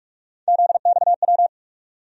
Morse code, heard as a keyed tone, about 46 wpm, 700 Hz. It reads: ZXW
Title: USA Call Sign Postfixes - 3 Letters 45wpm